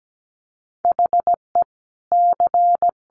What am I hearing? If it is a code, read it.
Morse code, heard as a keyed tone, about 17 wpm, 700 Hz. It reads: HE C
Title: Sentences from Top 1400 Words 17wpm